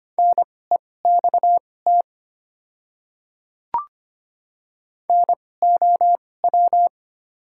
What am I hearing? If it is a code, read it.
Morse code, heard as a keyed tone, about 25 wpm, spaced as for 18 wpm, 690 Hz. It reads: NEXT NOW
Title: 2x Character Spacing - Top 500 Words - 25wpm